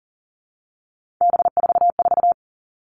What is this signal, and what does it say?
Morse code, heard as a keyed tone, about 40 wpm, 700 Hz. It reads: B44
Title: All Cut Numbers - 3-Digit 40wpm